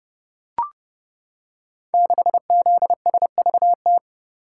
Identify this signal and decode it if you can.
Morse code, heard as a keyed tone, about 30 wpm, 700 Hz. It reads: 6ZSVT
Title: Code Groups - 5 Letter:Numbers - 30wpm